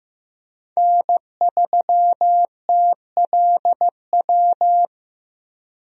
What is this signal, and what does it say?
Morse code, heard as a keyed tone, about 15 wpm, 700 Hz. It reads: N3TLW
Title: USA Call Signs - 5 Characters - Repeated 3x - 15wpm